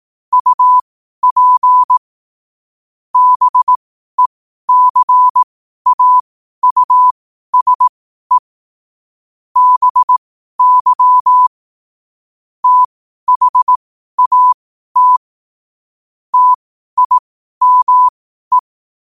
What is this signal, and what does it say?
Morse code, heard as a keyed tone, about 18 wpm, spaced as for 12 wpm, 1000 Hz. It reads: UP BECAUSE BY THAT TIME
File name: qotdfiles / 12wpm